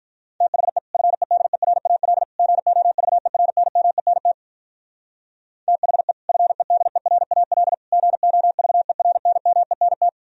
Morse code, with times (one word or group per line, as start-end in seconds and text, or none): 0.4-0.8 THE
0.9-2.2 FEDERAL
2.4-4.3 GOVERNMENT
5.7-6.1 THE
6.3-7.7 FEDERAL
7.9-10.1 GOVERNMENT